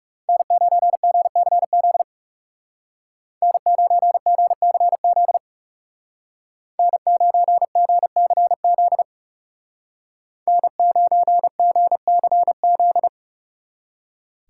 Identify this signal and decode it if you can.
Morse code, heard as a keyed tone, about 40 wpm, 700 Hz. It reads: N9GCZ N9GCZ N9GCZ N9GCZ